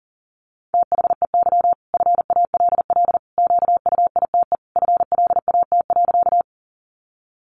Morse code, heard as a keyed tone, about 40 wpm, 700 Hz. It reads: THEY FALL QUITE FLAT.
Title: Sentences from Top 1000 Words 40wpm